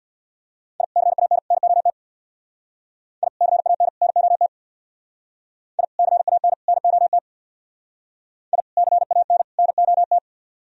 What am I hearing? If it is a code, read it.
Morse code, heard as a keyed tone, about 62 wpm, 700 Hz. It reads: I CAN NOT I CAN NOT I CAN NOT I CAN NOT
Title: Speed-Racing - Sets of 3 Words 50wpm